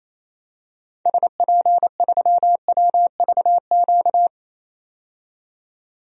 Morse code, heard as a keyed tone, about 28 wpm, 700 Hz. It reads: SP3WVQ
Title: DXCC-100 Call Signs - 6 Characters - 28wpm